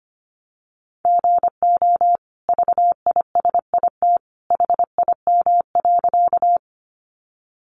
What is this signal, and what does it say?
Morse code, heard as a keyed tone, about 25 wpm, 700 Hz. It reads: GO VISIT HIM.
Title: Sentences from Top 900 Words 25wpm